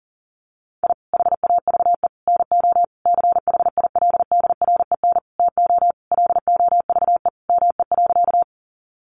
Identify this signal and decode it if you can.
Morse code, heard as a keyed tone, about 37 wpm, 700 Hz. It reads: I HAVE NO CHILDREN TO LOVE ME.
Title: Sentences from Top 400 Words 40wpm